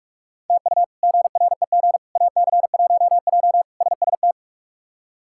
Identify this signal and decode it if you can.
Morse code, heard as a keyed tone, about 42 wpm, 700 Hz. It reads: TU GREG AC1J SST
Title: K1USN SST QSO Practice - 45wpm